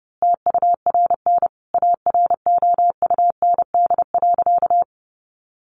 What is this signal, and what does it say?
Morse code, heard as a keyed tone, about 30 wpm, 700 Hz. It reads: TURN AROUND.
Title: A Sound Of Thunder 30wpm